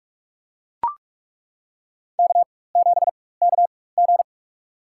Morse code, heard as a keyed tone, about 42 wpm, spaced as for 21 wpm, 700 Hz. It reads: K7KG
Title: USA Call Signs - 4 Characters - 4x Character Spacing - 45wpm